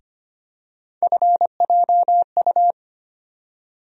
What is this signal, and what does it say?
Morse code, heard as a keyed tone, about 25 wpm, 700 Hz. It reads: FJU
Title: USA Call Sign Postfixes - 3 Letters 25wpm